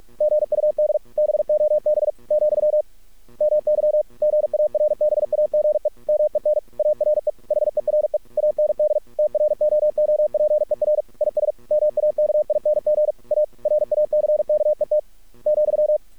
Morse code, heard as a keyed tone, about 46 wpm, 600 Hz. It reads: GUD BOB , MY GARBAGE MEN ARE HERE AND TROOPER IS MAKING A RACKET ,
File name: archived / CRSnet